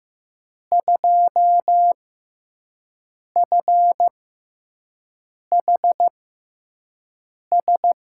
Morse code, heard as a keyed tone, about 15 wpm, spaced as for 5 wpm, 700 Hz. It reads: 2FHS
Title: USA Call Signs - 5 Characters - 6x Character Spacing - 15wpm